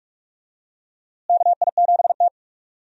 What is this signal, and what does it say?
Morse code, heard as a keyed tone, about 45 wpm, 700 Hz. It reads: KI7T
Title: USA Call Signs - 4 Characters - Repeated 3x - 45wpm